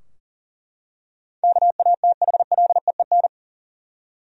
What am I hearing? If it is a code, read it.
Morse code, heard as a keyed tone, about 40 wpm, 700 Hz. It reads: KATHLEEN
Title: USA Common Names - 40wpm